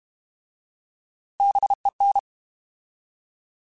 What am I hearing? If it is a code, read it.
Morse code, heard as a keyed tone, about 32 wpm, 800 Hz. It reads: BEN
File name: Alphabetical / Alphabetical mp3s / Bs @30